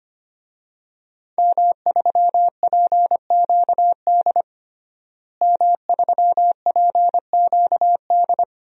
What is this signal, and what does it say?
Morse code, heard as a keyed tone, about 25 wpm, 700 Hz. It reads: M3PQD M3PQD
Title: DXCC-100 Call Signs - 5 Characters - Repeated 3x - 25wpm